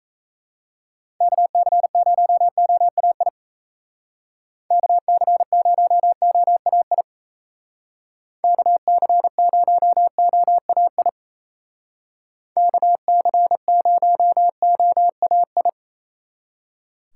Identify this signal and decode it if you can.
Morse code, heard as a keyed tone, about 33 wpm, 700 Hz. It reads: KC0OAI KC0OAI KC0OAI KC0OAI
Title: Speed Racing - USA 2x3 Call Signs 28wpm